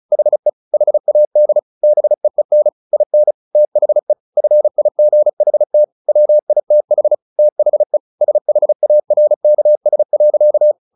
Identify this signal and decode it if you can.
Morse code, heard as a keyed tone, about 34 wpm, 600 Hz. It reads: HE HAD BEEN IN THE FIGHT WITH THE SHARKS.
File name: the old man and the sea 2 35wpm